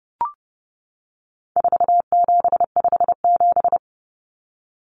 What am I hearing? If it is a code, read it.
Morse code, heard as a keyed tone, about 30 wpm, 700 Hz. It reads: 4757
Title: Numbers - 4-Digit 30wpm